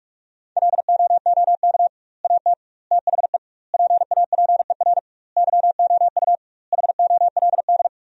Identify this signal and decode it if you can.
Morse code, heard as a keyed tone, about 46 wpm, 700 Hz. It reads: LOOK AT THE PAPER YOU HOLD
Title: 2x Word-Spacing - Sentences from Top 400 Words - 45wpm